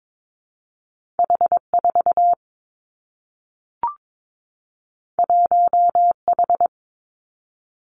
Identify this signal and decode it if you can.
Morse code, heard as a keyed tone, about 22 wpm, 700 Hz. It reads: H 4 1 H
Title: Two Challenging Letter-Number Combinations 22wpm